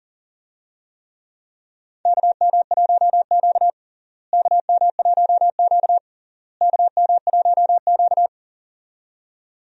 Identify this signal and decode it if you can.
Morse code, heard as a keyed tone, about 40 wpm, 700 Hz. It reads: KM1Q KM1Q KM1Q
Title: USA Call Signs - 4 Characters - Repeated 3x - 40wpm